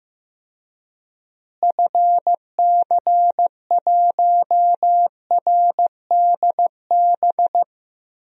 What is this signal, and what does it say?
Morse code, heard as a keyed tone, about 15 wpm, 700 Hz. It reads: FC1RDB
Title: DXCC-100 Call Signs - 6 Characters - Repeated 3x - 15wpm